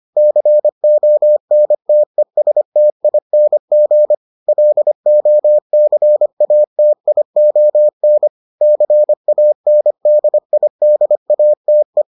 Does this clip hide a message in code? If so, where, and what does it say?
Yes: Morse code from 0.2 to 12.0 s, heard as a keyed tone, about 25 wpm, 600 Hz. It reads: CONTESTING LOCATION CANDIDATE